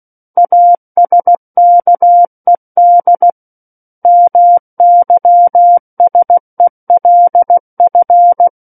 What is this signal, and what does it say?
Morse code, heard as a keyed tone, about 16 wpm, 700 Hz. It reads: ASKED MYSELF
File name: Book One / 07-Chapter 7 - 16 WPM